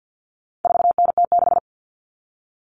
Morse code, heard as a keyed tone, about 50 wpm, 700 Hz. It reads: 4NT6
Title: Common Cut Numbers - 4-Digit 50wpm